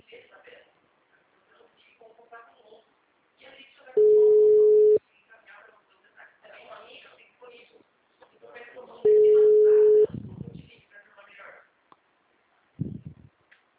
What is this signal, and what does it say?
Ringback tone (Europe, 425 Hz), with speech.
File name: out- -2001- - - .603